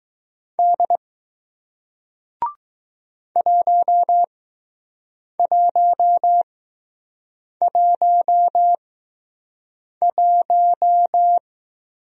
Morse code, heard as a keyed tone, about 20 wpm, spaced as for 6 wpm, 700 Hz. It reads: D 1111